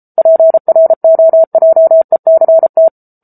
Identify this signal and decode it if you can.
Morse code, heard as a keyed tone, about 34 wpm, 650 Hz. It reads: PROJECT
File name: Book Two / 07-Chapter 7 - 33 WPM